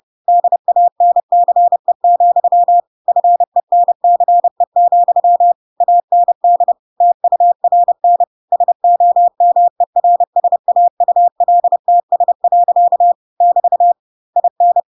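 Morse code, heard as a keyed tone, about 30 wpm, 700 Hz. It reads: DANCE, FENCE, AND TURN SOMERSAULTS. <BT> IN
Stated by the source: Chapter 2 of The Adventures of Pinocchio read as a book in Morse code (CW) at 30 wpm, by ik1hge.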